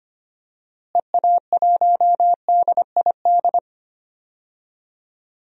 Morse code, heard as a keyed tone, about 25 wpm, 700 Hz. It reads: EA1DID
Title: DXCC-100 Call Signs - 6 Characters - Repeated 2x - 25wpm